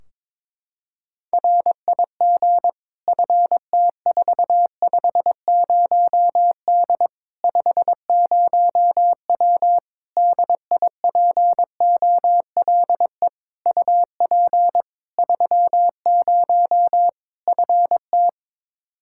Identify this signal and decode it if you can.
Morse code, heard as a keyed tone, about 22 wpm, 700 Hz. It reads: RIG FT450D 50W DIPOLE UP 30 FT
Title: Practice QSO - Exchange 2 - Info - 22wpm